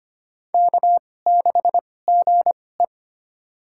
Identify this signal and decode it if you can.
Morse code, heard as a keyed tone, about 25 wpm, spaced as for 18 wpm, 700 Hz. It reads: K6GE
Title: USA Call Signs - 4 Characters - 2x Character Spacing - 25wpm